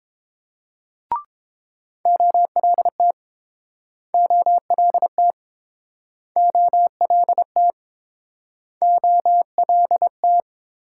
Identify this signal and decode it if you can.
Morse code, heard as a keyed tone, about 30 wpm, 700 Hz. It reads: OLT OLT OLT OLT